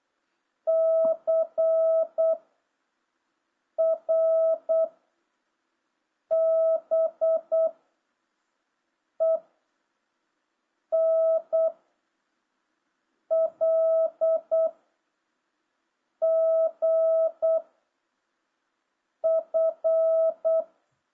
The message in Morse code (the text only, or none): C R B E N L G F